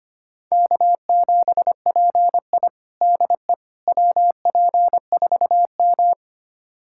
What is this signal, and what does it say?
Morse code, heard as a keyed tone, about 25 wpm, 700 Hz. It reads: K7PI DE WP4M
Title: USA Callsign DE Callsign - 4 Characters - 25wpm